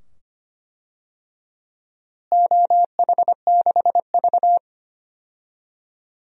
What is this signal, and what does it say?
Morse code, heard as a keyed tone, about 25 wpm, 700 Hz. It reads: OH6V
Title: DXCC-100 Call Signs - 4 Characters - Repeated 2x - 25wpm